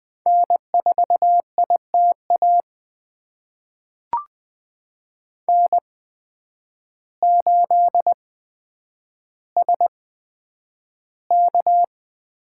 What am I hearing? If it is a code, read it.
Morse code, heard as a keyed tone, about 20 wpm, spaced as for 9 wpm, 700 Hz. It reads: N4ITA N8SK
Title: USA Call Signs - 5 Characters - 8x Character Spacing - 20wpm